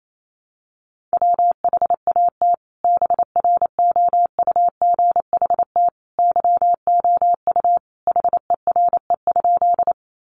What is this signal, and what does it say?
Morse code, heard as a keyed tone, about 28 wpm, 700 Hz. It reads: WHAT BROUGHT YOU HERE?
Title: Sentences from Top 500 Words 28wpm